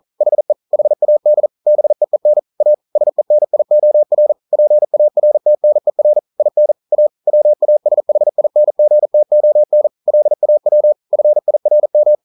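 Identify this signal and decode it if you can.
Morse code, heard as a keyed tone, about 38 wpm, 600 Hz. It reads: HE HAD BEEN A SENIOR PARTNER IN A WASHINGTON LAW FIRM
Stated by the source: Morse code practice at 41 wpm, from CW med Gnister.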